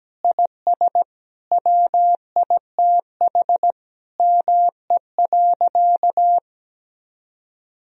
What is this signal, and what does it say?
Morse code, heard as a keyed tone, about 17 wpm, 700 Hz. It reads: IS WITH ME.
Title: Sentences from Top 1000 Words - Review Part 3 - 17wpm